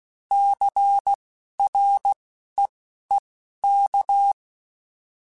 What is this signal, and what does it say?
Morse code, heard as a keyed tone, about 16 wpm, spaced as for 11 wpm, 800 Hz. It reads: CREEK